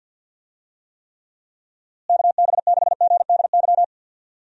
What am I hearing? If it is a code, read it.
Morse code, heard as a keyed tone, about 52 wpm, 700 Hz. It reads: KB6GDY
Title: USA Call Signs - 6 Characters - Repeated 3x - 50wpm